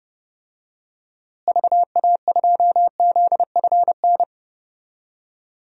Morse code, heard as a keyed tone, about 30 wpm, 700 Hz. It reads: VA2ZFN
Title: DXCC-100 Call Signs - 6 Characters - Repeated 3x - 30wpm